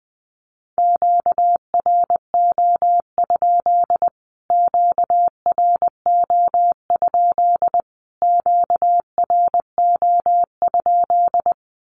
Morse code, heard as a keyed tone, about 20 wpm, 700 Hz. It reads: QRO? QRO? QRO?
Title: Q-Codes Repeated 3x 20wpm